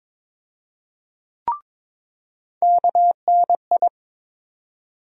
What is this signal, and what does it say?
Morse code, heard as a keyed tone, about 22 wpm, 700 Hz. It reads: KNI
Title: USA Call Sign Postfixes - 3 Letters 22wpm